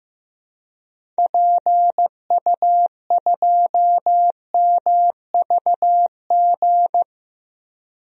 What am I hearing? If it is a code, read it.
Morse code, heard as a keyed tone, about 15 wpm, 700 Hz. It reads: PU2MVG
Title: DXCC-100 Call Signs - 6 Characters - Repeated 2x - 15wpm